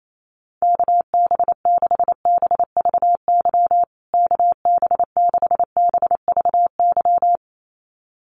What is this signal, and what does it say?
Morse code, heard as a keyed tone, about 28 wpm, 700 Hz. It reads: KB6BVY KB6BVY
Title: USA Call Signs Repeated 2x 28wpm